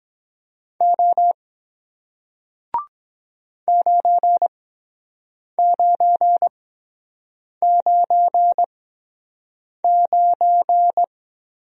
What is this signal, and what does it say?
Morse code, heard as a keyed tone, about 23 wpm, spaced as for 7 wpm, 700 Hz. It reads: O 9999